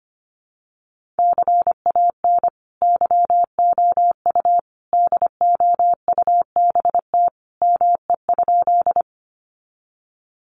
Morse code, heard as a keyed tone, about 25 wpm, 700 Hz. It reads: CAN YOU DOUBT ME?
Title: Sentences from Top 1500 Words 25wpm